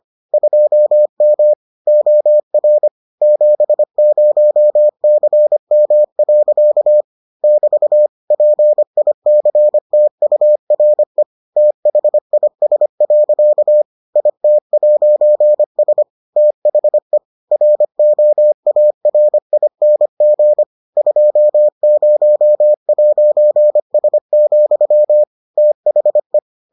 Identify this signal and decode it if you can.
Morse code, heard as a keyed tone, about 25 wpm, 600 Hz. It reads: 2M OR 70CM. <BT> PICTURE THIS. IT'S THE ROARING 20'S, THE